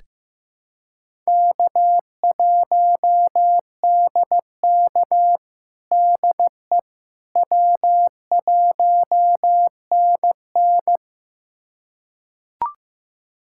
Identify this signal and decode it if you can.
Morse code, heard as a keyed tone, about 15 wpm, 700 Hz. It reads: K1DK DE W1NN
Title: USA Callsign DE Callsign - 4 Characters - 15wpm